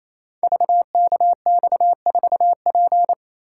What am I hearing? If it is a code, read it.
Morse code, heard as a keyed tone, about 29 wpm, 700 Hz. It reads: VKX4P
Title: Code Groups - 5 Letter:Numbers - 28wpm